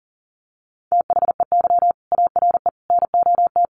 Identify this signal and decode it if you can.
Morse code, heard as a keyed tone, about 40 wpm, 700 Hz. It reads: THEY ARE NOT
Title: Sentences from Top 1200 Words 40wpm